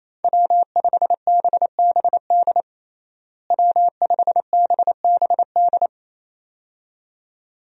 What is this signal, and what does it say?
Morse code, heard as a keyed tone, about 28 wpm, 700 Hz. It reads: W5BBD W5BBD
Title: USA Call Signs - 5 Characters - Repeated 2x - 28wpm